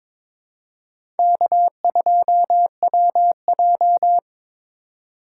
Morse code, heard as a keyed tone, about 21 wpm, 700 Hz. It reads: K2WJ
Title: USA Call Signs - 4 Characters - 2x Character Spacing - 22wpm